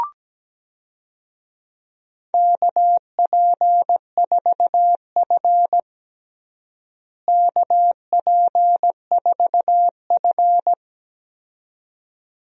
Morse code, heard as a keyed tone, about 17 wpm, 700 Hz. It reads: KP4F KP4F
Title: DXCC-100 Call Signs - 4 Characters - Repeated 2x - 17wpm